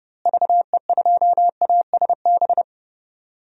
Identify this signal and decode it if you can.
Morse code, heard as a keyed tone, about 31 wpm, 700 Hz. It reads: VE2ASB
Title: DXCC-100 Call Signs - 6 Characters - Repeated 3x - 30wpm